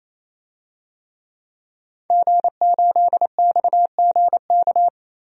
Morse code, heard as a keyed tone, about 28 wpm, 700 Hz. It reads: G8XGK
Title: DXCC-100 Call Signs - 5 Characters - 28wpm